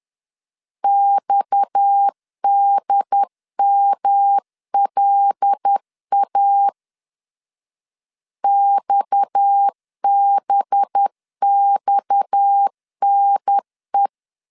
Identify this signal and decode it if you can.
Morse code, heard as a keyed tone, about 10 wpm, 800 Hz. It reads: XDMLA XBXNE
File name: LEZIONI / 6 PKX0 / 6 PKX 10wpm 2426